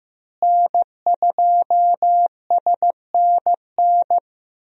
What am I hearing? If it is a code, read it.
Morse code, heard as a keyed tone, about 16 wpm, 700 Hz. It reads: N2SNN
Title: USA Call Signs - 5 Characters - Repeated 3x - 15wpm